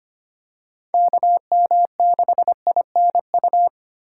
Morse code, heard as a keyed tone, about 25 wpm, 700 Hz. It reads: KM6INU
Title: USA Call Signs - 3x Character Spacing - 25wpm